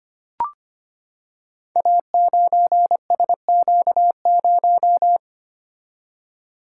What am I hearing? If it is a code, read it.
Morse code, heard as a keyed tone, about 25 wpm, 700 Hz. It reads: A9SQ0
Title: Code Groups - 5 Letter:Numbers - 25wpm